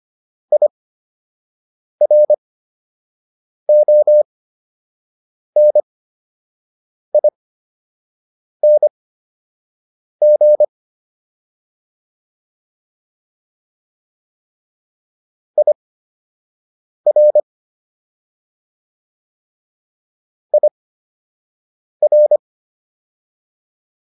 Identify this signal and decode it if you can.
Morse code, heard as a keyed tone, about 25 wpm, spaced as for 6 wpm, 600 Hz. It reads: IRONING IR IR